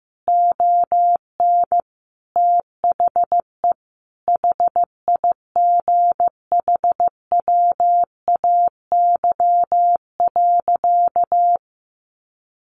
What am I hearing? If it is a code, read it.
Morse code, heard as a keyed tone, about 14 wpm, 700 Hz. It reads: ON THE HIGHWAY.
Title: Sentences from Top 1400 Words 15wpm